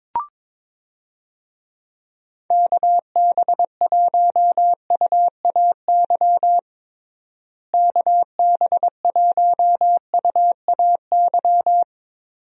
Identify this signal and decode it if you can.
Morse code, heard as a keyed tone, about 22 wpm, 700 Hz. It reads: KB1UAY KB1UAY
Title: USA Call Signs - 6 Characters - Repeated 2x - 22wpm